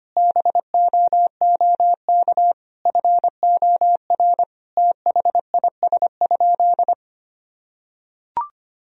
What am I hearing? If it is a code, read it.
Morse code, heard as a keyed tone, about 25 wpm, 700 Hz. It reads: BOOK FOR THIS?
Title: Sentences from Top 1000 Words - Review Part 4 - 25wpm